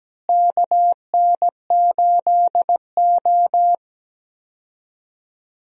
Morse code, heard as a keyed tone, about 17 wpm, 700 Hz. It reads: KN8O
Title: USA Call Signs - 4 Characters - Repeated 2x - 17wpm